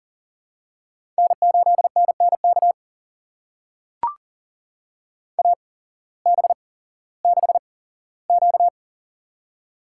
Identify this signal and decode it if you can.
Morse code, heard as a keyed tone, about 40 wpm, spaced as for 19 wpm, 700 Hz. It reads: N8NNK AB6Q
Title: USA Call Signs - 8x Character Spacing - 40wpm